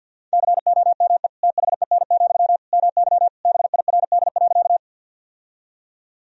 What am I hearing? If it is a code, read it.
Morse code, heard as a keyed tone, about 52 wpm, 700 Hz. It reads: COME THEN, MY BIRD.
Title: Sentences from Top 1000 Words - Review Part 2 - 50wpm